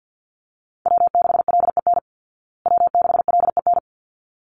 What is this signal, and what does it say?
Morse code, heard as a keyed tone, about 50 wpm, 700 Hz. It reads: W6LEN W6LEN
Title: Call Signs Repeated 3x - 50wpm